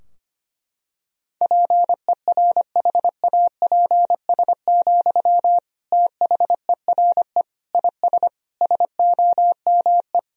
Morse code, heard as a keyed tone, about 25 wpm, 700 Hz. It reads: PERHAPS, THERE IS SOME
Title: Sentences from Top 1000 Words - Review Part 4 - 25wpm